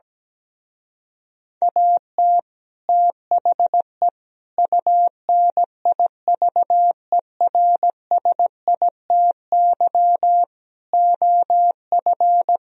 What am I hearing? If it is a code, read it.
Morse code, heard as a keyed tone, about 17 wpm, 700 Hz. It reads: AT THE UNIVERSITY OF
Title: Speed-Racing - Sets of 4 Words 17wpm